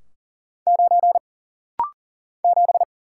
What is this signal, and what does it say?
Morse code, heard as a keyed tone, about 40 wpm, 700 Hz. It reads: 9 7
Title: Single Letter-Number Alt - Rapid-Fire 40wpm